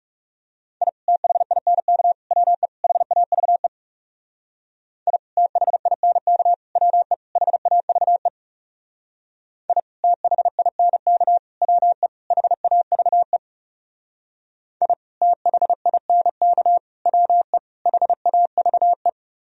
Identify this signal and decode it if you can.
Morse code, heard as a keyed tone, about 38 wpm, 700 Hz. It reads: I THINK WE HAVE I THINK WE HAVE I THINK WE HAVE I THINK WE HAVE